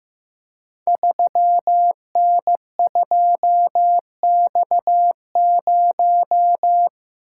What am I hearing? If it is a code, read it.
Morse code, heard as a keyed tone, about 15 wpm, 700 Hz. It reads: 3N2X0
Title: Code Groups - 5 Letter:Numbers - 15wpm